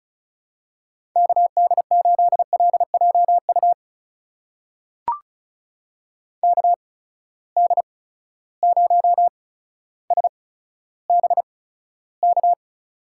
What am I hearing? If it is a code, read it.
Morse code, heard as a keyed tone, about 36 wpm, spaced as for 15 wpm, 700 Hz. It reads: KD8LJU KD0SBK
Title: USA Call Signs - 6 Characters - 8x Character Spacing - 35wpm